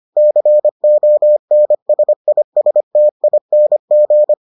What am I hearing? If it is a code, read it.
Morse code, heard as a keyed tone, about 25 wpm, 600 Hz. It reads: CONSISTING